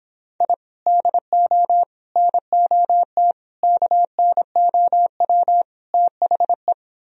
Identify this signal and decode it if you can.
Morse code, heard as a keyed tone, about 26 wpm, 700 Hz. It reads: I DO NOT KNOW THE